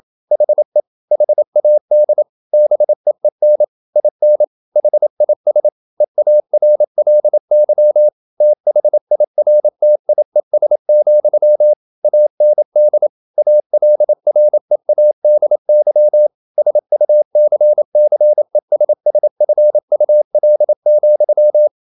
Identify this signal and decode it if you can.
Morse code, heard as a keyed tone, about 28 wpm, 600 Hz. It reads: HE HAD BEEN IN HIS EARLY THIRTIES, AND ALREADY SUCCESSFUL,
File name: Never 0027wpm